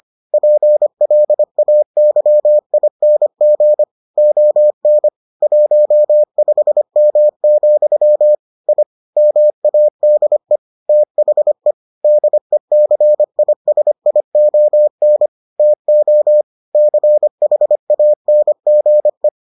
Morse code, heard as a keyed tone, about 25 wpm, 600 Hz. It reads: PLAYING ON 15M, I MADE THE DECISION TO CHANGE